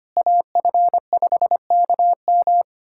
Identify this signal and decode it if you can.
Morse code, heard as a keyed tone, about 25 wpm, 700 Hz. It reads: AF5KM
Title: USA Callsign DE Callsign 25wpm